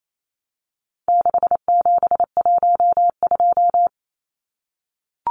Morse code, heard as a keyed tone, about 28 wpm, 700 Hz. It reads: 6712
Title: Numbers - 4-Digit 28wpm